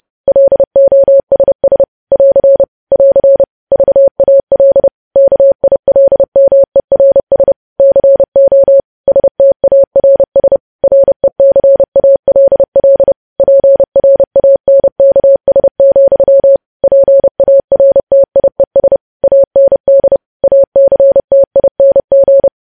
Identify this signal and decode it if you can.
Morse code, heard as a keyed tone, about 30 wpm, 550 Hz. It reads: LOSS <AR> <AR> VAL KILMERS CO STARS RECALL PRANKS, PARTIES AND ACTING
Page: news headlines in morse code at 30 wpm